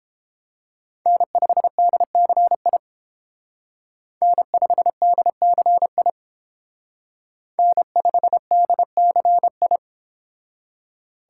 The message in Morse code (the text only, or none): N5DCI N5DCI N5DCI